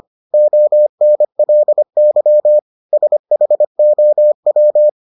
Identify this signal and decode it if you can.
Morse code, heard as a keyed tone, about 25 wpm, 600 Hz. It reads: ONLY SHOW